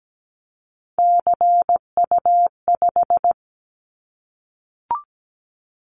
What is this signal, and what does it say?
Morse code, heard as a keyed tone, about 17 wpm, 700 Hz. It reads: CU5
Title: Callsign Prefixes for Top 50 DXCC - 17wpm